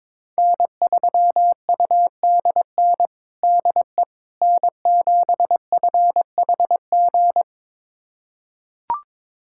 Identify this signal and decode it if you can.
Morse code, heard as a keyed tone, about 22 wpm, 700 Hz. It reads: N3UDN DE N7FHG
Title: USA Callsign DE Callsign - 5 Characters - 22wpm